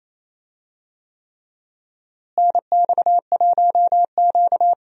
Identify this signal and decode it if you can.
Morse code, heard as a keyed tone, about 28 wpm, 700 Hz. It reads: NX1Q
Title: USA Call Signs - 4 Characters - Repeated 3x - 28wpm